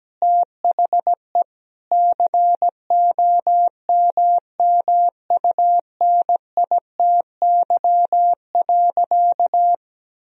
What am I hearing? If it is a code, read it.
Morse code, heard as a keyed tone, about 17 wpm, 700 Hz. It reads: THE COMMUNITY.
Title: Sentences from Top 1000 Words - Review Part 4 - 17wpm